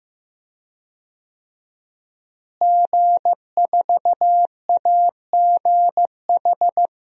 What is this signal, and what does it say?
Morse code, heard as a keyed tone, about 15 wpm, 700 Hz. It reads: G4AGH
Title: DXCC-100 Call Signs - 5 Characters - Repeated 3x - 15wpm